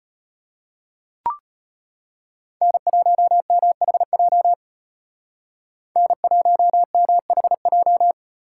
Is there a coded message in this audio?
Yes: Morse code from 2.6 to 8.1 s, heard as a keyed tone, about 33 wpm, 700 Hz. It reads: N1MHJ N1MHJ